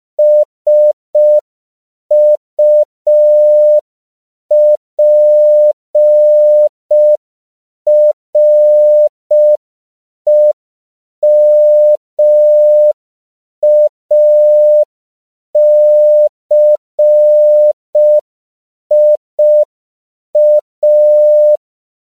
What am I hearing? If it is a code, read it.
Morse code, heard as a keyed tone, about 5 wpm, 600 Hz. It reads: SUPREMACIA
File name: CW / 5PPM